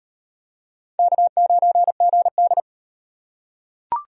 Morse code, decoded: K9GD

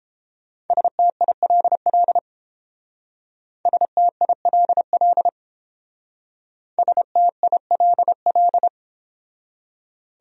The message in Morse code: STILL STILL STILL

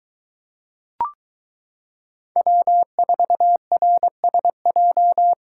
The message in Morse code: W4RSJ

